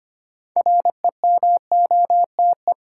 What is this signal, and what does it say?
Morse code, heard as a keyed tone, about 25 wpm, 700 Hz. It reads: REMOTE
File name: practice / 25wpm / Difficult Words with Hints - 25wpm